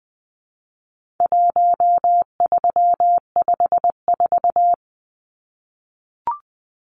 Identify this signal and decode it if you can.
Morse code, heard as a keyed tone, about 20 wpm, 700 Hz. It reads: 1354
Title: Numbers - 4-Digit 20wpm